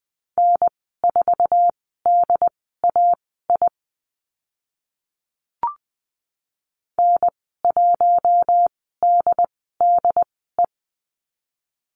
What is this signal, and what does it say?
Morse code, heard as a keyed tone, about 20 wpm, spaced as for 14 wpm, 700 Hz. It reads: N4DAI N1DDE
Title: 069 Lesson 12g - Letter d - Call Signs 20wpm